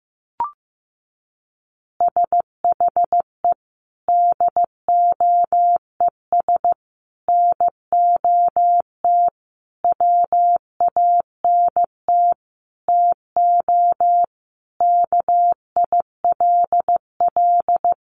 Morse code, heard as a keyed tone, about 15 wpm, 700 Hz. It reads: SHE DOES NOT WANT TO KILL
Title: Easy Phrases 15wpm